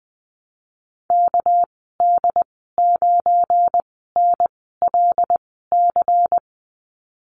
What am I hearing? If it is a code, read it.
Morse code, heard as a keyed tone, about 20 wpm, spaced as for 14 wpm, 700 Hz. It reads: KD9NLC
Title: 219 Lesson 35g - Number 0 - Call Signs 20wpm